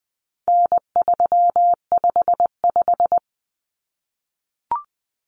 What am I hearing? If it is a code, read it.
Morse code, heard as a keyed tone, about 20 wpm, 700 Hz. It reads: N355
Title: Common Cut Numbers - 4-Digit 20wpm